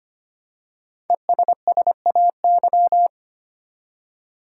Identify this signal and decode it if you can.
Morse code, heard as a keyed tone, about 25 wpm, 700 Hz. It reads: ESSAY
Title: Five Letter Words - 25wpm